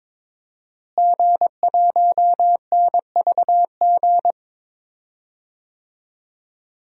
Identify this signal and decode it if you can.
Morse code, heard as a keyed tone, about 22 wpm, 700 Hz. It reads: G1NVG
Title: DXCC-100 Call Signs - 5 Characters - Repeated 2x - 22wpm